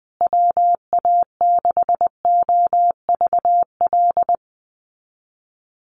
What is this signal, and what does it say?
Morse code, heard as a keyed tone, about 20 wpm, 700 Hz. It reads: WA6OVL
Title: USA Call Signs Repeated 2x 20wpm